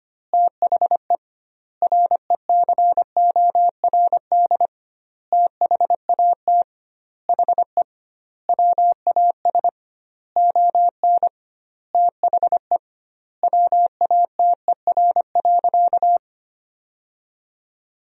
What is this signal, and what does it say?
Morse code, heard as a keyed tone, about 25 wpm, 700 Hz. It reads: THE RECORD THAT HE WAS ON THE WATER.
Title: 2x Word-Spacing - Sentences from Top 500 Words - 25wpm